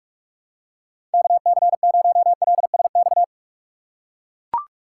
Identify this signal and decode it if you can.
Morse code, heard as a keyed tone, about 43 wpm, 700 Hz. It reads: KC0LSX
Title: USA Call Signs - 2x Character Spacing - 45wpm